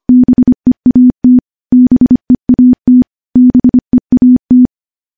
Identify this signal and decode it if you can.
Morse code, heard as a keyed tone, about 25 wpm, 250 Hz. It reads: BEAT BEAT BEAT
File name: morse / morse-training-25WPM